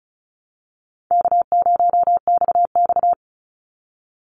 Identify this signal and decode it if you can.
Morse code, heard as a keyed tone, about 34 wpm, 700 Hz. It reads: K0XX
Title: Easy Call Signs Repeated 3x - 35wpm